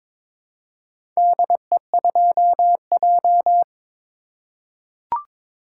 Morse code, heard as a keyed tone, about 22 wpm, 700 Hz. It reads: DE2J
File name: practice / 22wpm / Four Letter-Number Combinations - 22wpm